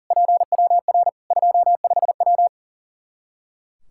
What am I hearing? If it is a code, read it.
Morse code, heard as a keyed tone, about 40 wpm, 700 Hz. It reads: PWR 25W
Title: K1USN SST QSO Practice 40wpm